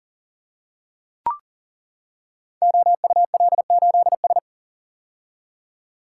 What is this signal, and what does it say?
Morse code, heard as a keyed tone, about 39 wpm, 700 Hz. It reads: OUL8S
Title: Code Groups - 5 Letter:Numbers - 40wpm